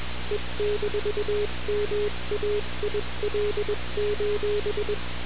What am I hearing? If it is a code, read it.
Morse code, heard as a keyed tone, about 21 wpm, 420 Hz, with noise. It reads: E-MAIL:
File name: radiocap / REC z